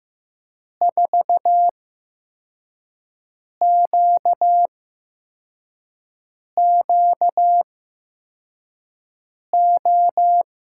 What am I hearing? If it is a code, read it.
Morse code, heard as a keyed tone, about 15 wpm, spaced as for 4 wpm, 700 Hz. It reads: 4QQO
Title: USA Call Signs - 6 Characters - 8x Character Spacing - 15wpm